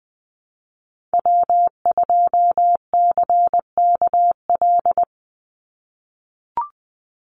Morse code, heard as a keyed tone, about 20 wpm, 700 Hz. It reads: W2CKL
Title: NA SOTA Activator Callsigns 20wpm